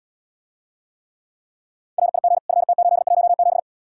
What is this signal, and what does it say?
Morse code, heard as a keyed tone, about 79 wpm, 700 Hz. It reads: RIG FT817